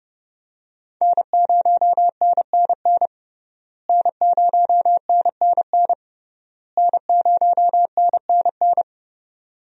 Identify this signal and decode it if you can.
Morse code, heard as a keyed tone, about 30 wpm, 700 Hz. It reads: N0NNN N0NNN N0NNN
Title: USA Call Signs - 5 Characters - Repeated 3x - 30wpm